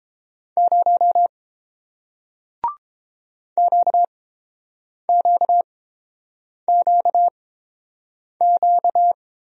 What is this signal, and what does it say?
Morse code, heard as a keyed tone, about 30 wpm, spaced as for 7 wpm, 700 Hz. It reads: 0 QQQQ